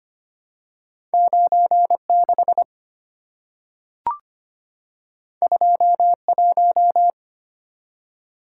Morse code, heard as a keyed tone, about 25 wpm, 700 Hz. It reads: 96 21
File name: Two Letter-Number Combinations - 25wpm